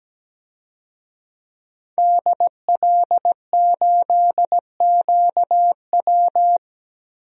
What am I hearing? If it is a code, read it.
Morse code, heard as a keyed tone, about 17 wpm, 700 Hz. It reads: DL8QW
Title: DXCC-100 Call Signs - 5 Characters - 17wpm